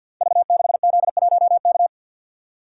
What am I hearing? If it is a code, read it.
Morse code, heard as a keyed tone, about 48 wpm, 700 Hz. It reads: V6Z1K
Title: Code Groups - 5 Letter:Numbers - 50wpm